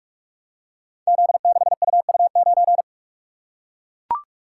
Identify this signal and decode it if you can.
Morse code, heard as a keyed tone, about 44 wpm, 700 Hz. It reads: Z6UU9
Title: Code Groups - 5 Letter:Numbers - 45wpm